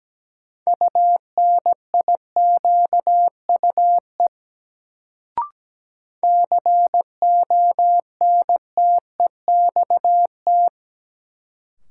Morse code, heard as a keyed tone, about 17 wpm, 700 Hz. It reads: UNIQUE CONTEXT
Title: Sentences from Top 1000 Words - Review Part 7 - New Word Drill - 17wpm